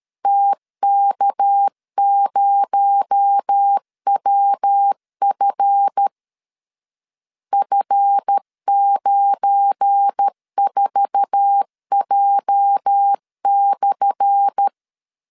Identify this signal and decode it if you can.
Morse code, heard as a keyed tone, about 13 wpm, 800 Hz. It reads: TK0WF F94J/
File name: LEZIONI / 10 SIGNS / 10D COMA 12wpm 2544